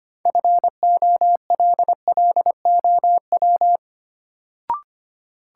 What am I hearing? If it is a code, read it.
Morse code, heard as a keyed tone, about 25 wpm, 700 Hz. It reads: FOLLOW
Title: Sentences from Top 200 Words - New Word Drill - 25wpm